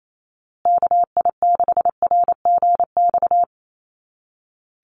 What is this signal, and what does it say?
Morse code, heard as a keyed tone, about 28 wpm, 700 Hz. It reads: KI6RGX
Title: USA Call Signs Repeated 2x 28wpm